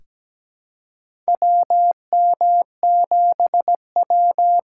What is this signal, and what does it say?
Morse code, heard as a keyed tone, about 17 wpm, 700 Hz. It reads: WM7W
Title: USA Call Signs - 4 Characters - 2x Character Spacing - 17wpm